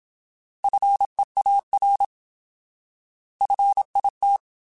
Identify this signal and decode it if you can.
Morse code, heard as a keyed tone, about 27 wpm, 800 Hz. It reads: FEAR FIT
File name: Alphabetical / Alphabetical mp3s / Fs @25